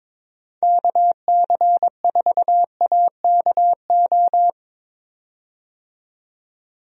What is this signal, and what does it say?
Morse code, heard as a keyed tone, about 22 wpm, 700 Hz. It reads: KC4AKO
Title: USA Call Signs - 6 Characters - 22wpm